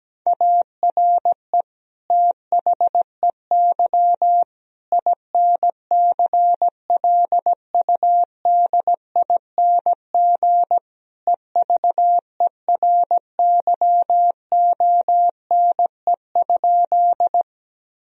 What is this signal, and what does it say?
Morse code, heard as a keyed tone, about 17 wpm, 700 Hz. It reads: ARE THEY INCLUDING EVERYONE?
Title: Sentences from Top 1000 Words - Review Part 4 - 17wpm